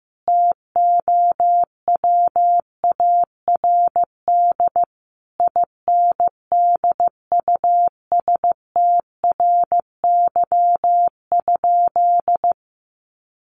Morse code, heard as a keyed tone, about 15 wpm, 700 Hz. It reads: TOWARD INDUSTRY?
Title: Sentences from Top 1300 Words 15wpm